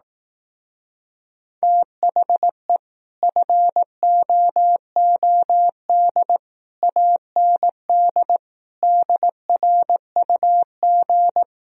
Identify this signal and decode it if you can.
Morse code, heard as a keyed tone, about 18 wpm, 700 Hz. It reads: THE FOOD AND DRUG
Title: Speed-Racing - Sets of 5 Words 15wpm